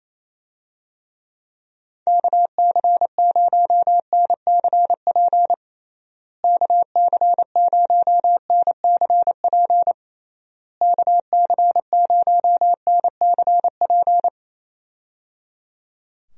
Morse code, heard as a keyed tone, about 28 wpm, 700 Hz. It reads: KC0NCP KC0NCP KC0NCP
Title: USA Call Signs - 6 Characters - Repeated 3x - 28wpm